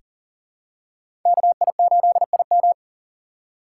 Morse code, heard as a keyed tone, about 40 wpm, 700 Hz. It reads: KI8IM
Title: USA Call Signs - 5 Characters - 4x Character Spacing - 40wpm